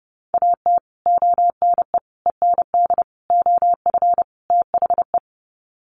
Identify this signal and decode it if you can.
Morse code, heard as a keyed tone, about 30 wpm, 700 Hz. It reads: AT ONE END OF THE
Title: Sets of 5 Words Encore 30wpm